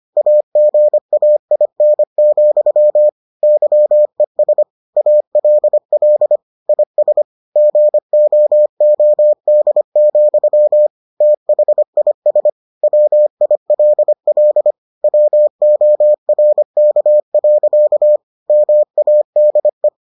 Morse code, heard as a keyed tone, about 25 wpm, 600 Hz. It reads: AGAIN, YES ALL IS GOOD, THIS WILL WORK. MADE